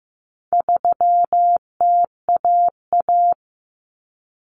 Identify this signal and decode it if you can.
Morse code, heard as a keyed tone, about 15 wpm, 700 Hz. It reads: 3TAA
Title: Common Cut Numbers - 4-Digit 15wpm